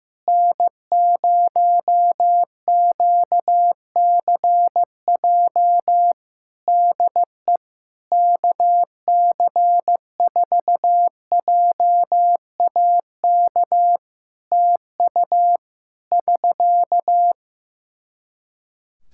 Morse code, heard as a keyed tone, about 15 wpm, 700 Hz. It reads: N0QCJ DE KC4JAK TU <SK>
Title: K1USN SST QSO Practice 15wpm